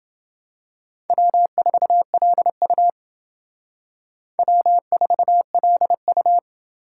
Morse code, heard as a keyed tone, about 28 wpm, 700 Hz. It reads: W4LU W4LU